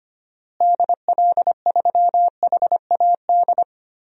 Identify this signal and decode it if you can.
Morse code, heard as a keyed tone, about 25 wpm, 700 Hz. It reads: DL3HAD
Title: DXCC-100 Call Signs - 6 Characters - Repeated 2x - 25wpm